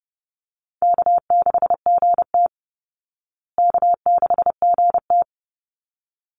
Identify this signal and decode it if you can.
Morse code, heard as a keyed tone, about 30 wpm, 700 Hz. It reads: K6GT K6GT
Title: Easy Call Signs Repeated 2x - 30wpm